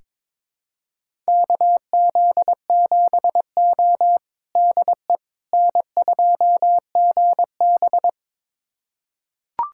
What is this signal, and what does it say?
Morse code, heard as a keyed tone, about 22 wpm, 700 Hz. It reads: KZ7O DE N2GB
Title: USA Callsign DE Callsign - 4 Characters - 22wpm